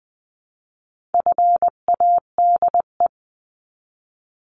Morse code, heard as a keyed tone, about 20 wpm, 700 Hz. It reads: FADE
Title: 259 Lesson 42d - Words 18-20wpm